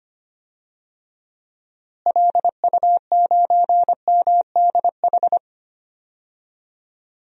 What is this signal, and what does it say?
Morse code, heard as a keyed tone, about 25 wpm, 700 Hz. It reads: LU9MDH
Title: DXCC-100 Call Signs - 6 Characters - Repeated 2x - 25wpm